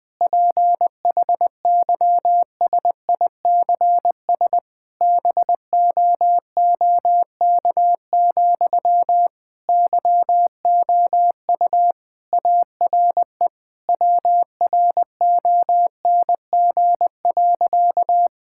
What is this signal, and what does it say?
Morse code, heard as a keyed tone, about 21 wpm, 700 Hz. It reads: PHYSICS BOOK, YOU ARE WRONG.
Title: Sentences from Top 1000 Words - Review Part 5 - 20wpm